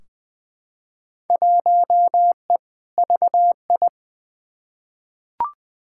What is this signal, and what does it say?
Morse code, heard as a keyed tone, about 20 wpm, 700 Hz. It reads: 1E VI
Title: Field Day Exchanges Encore - 20wpm